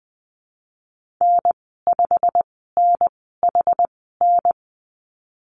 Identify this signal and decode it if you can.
Morse code, heard as a keyed tone, about 20 wpm, spaced as for 15 wpm, 700 Hz. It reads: N5NHN
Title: 219 Lesson 35g - Number 0 - Call Signs 20wpm